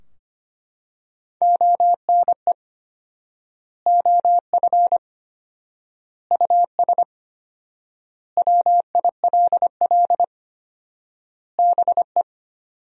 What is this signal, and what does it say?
Morse code, heard as a keyed tone, about 25 wpm, 700 Hz. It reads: ONE OF US WILL BE
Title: 4x Word-Spacing - Sentences from Top 300 Words - 25wpm